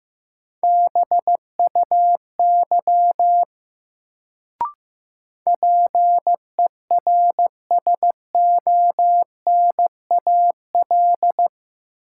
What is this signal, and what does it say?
Morse code, heard as a keyed tone, about 15 wpm, 700 Hz. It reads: BUY PERSONAL
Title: Sentences from Top 700 Words - New Word Drill - 15wpm